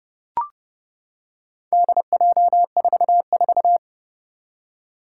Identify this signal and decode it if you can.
Morse code, heard as a keyed tone, about 30 wpm, 700 Hz. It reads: DJ44
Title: Four Challenging Letter-Number Combinations 30wpm